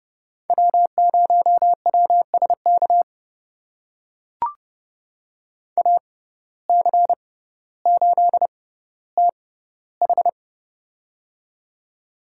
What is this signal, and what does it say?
Morse code, heard as a keyed tone, about 30 wpm, spaced as for 15 wpm, 700 Hz. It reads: W0WSK AC8TH
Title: USA Call Signs - 6x Character Spacing - 30wpm